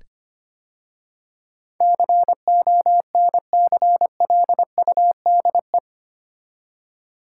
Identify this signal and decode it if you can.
Morse code, heard as a keyed tone, about 25 wpm, 690 Hz. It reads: CONCLUDE
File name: 8 Letter Words - First 3 Letters Given - 25wpm